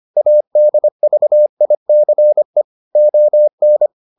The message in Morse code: ADVICE ON

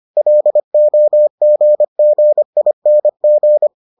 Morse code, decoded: LOGGING

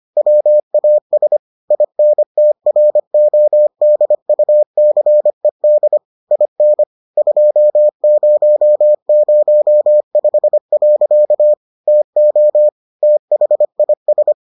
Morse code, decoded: WAS INTRODUCED IN 2005. TO THIS

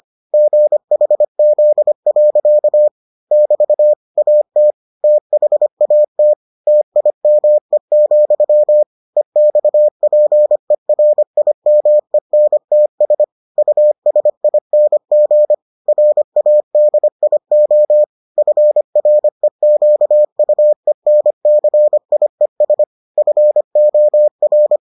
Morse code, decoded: GHZ. <BT> AT THAT TIME, EXPERIMENTS USING RADIO FREQUENCIES FOR

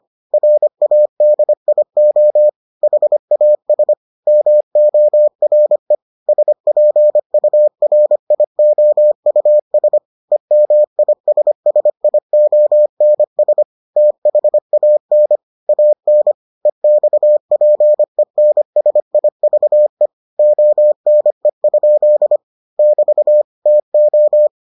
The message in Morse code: RADIO HAS MORE SPURIOUS EMISSIONS THAN AN EXPENSIVE ONE? <BT> TO

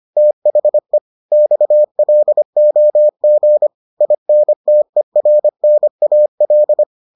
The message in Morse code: THE XLOG INTERNAL

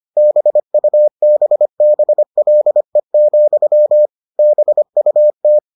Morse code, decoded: BUBBLE, BUT

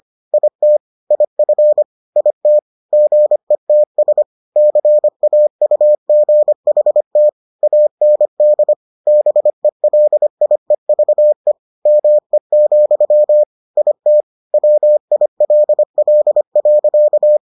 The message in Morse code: IT IF IT GETS CAUGHT AND BELIEVE ME, IT WILL.

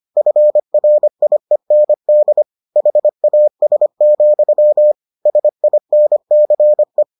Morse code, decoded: FRIEND HAS, SINCE

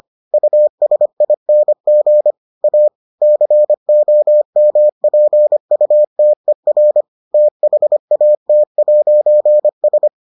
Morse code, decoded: USING A COMPUTER THAT'S

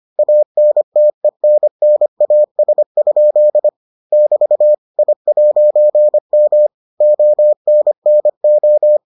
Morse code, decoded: ANTENNAS? <BT> I'M ONNO

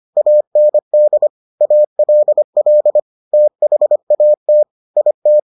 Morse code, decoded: AND ALL THAT IT

